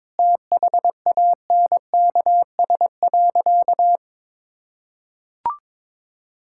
Morse code, heard as a keyed tone, about 21 wpm, 700 Hz. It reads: THANKS.
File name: practice / 22wpm / Sentences with 4-Word Phrases - 22wpm